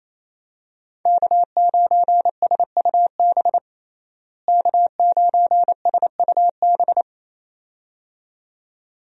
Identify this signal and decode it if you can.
Morse code, heard as a keyed tone, about 27 wpm, 700 Hz. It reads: K9SUB K9SUB
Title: USA Call Signs - 5 Characters - Repeated 2x - 28wpm